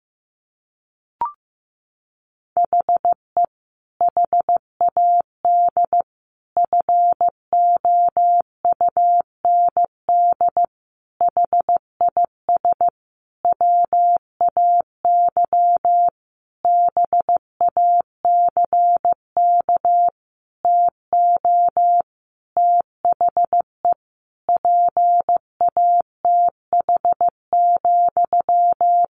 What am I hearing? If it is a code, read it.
Morse code, heard as a keyed tone, about 15 wpm, 700 Hz. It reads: HE HAD FOUND HIS WAY BACK TO THE PATH,
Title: A Sound Of Thunder 15wpm